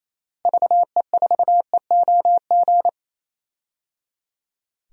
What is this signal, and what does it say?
Morse code, heard as a keyed tone, about 28 wpm, 700 Hz. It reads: VE4EOG
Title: DXCC-100 Call Signs - 6 Characters - Repeated 3x - 28wpm